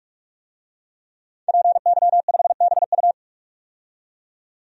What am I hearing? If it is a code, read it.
Morse code, heard as a keyed tone, about 44 wpm, 690 Hz. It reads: PY5BU
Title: DXCC-100 Call Signs - 5 Characters - 45wpm